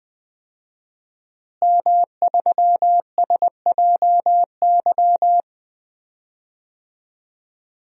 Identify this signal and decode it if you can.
Morse code, heard as a keyed tone, about 20 wpm, 700 Hz. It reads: M3SJY
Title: DXCC-100 Call Signs - 5 Characters - Repeated 3x - 20wpm